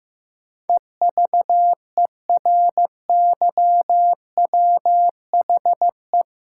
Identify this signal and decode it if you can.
Morse code, heard as a keyed tone, about 16 wpm, 700 Hz. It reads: EVERYWHE
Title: Sentences from Top 1600 Words - New Word Drill - 15wpm